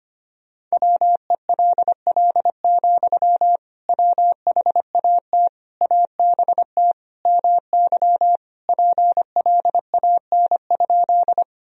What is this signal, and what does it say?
Morse code, heard as a keyed tone, about 25 wpm, 700 Hz. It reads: WELL, WHAT ABT MY PLAN?
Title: Sentences from Top 500 Words and Abbreviations 25wpm